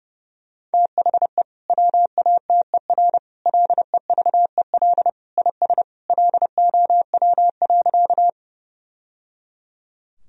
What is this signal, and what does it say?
Morse code, heard as a keyed tone, about 30 wpm, 700 Hz. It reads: THE WATER LEVEL IS LOW.
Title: Sentences from Top 1000 Words - Review Part 4 - 30wpm